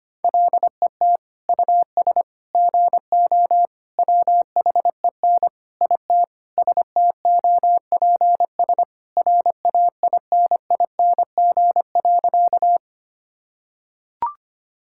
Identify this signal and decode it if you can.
Morse code, heard as a keyed tone, about 25 wpm, 700 Hz. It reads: LET US GO WHEN IT STOPS RAINING.
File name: practice / 25wpm / Sentences with 2-Word Phrases - 25wpm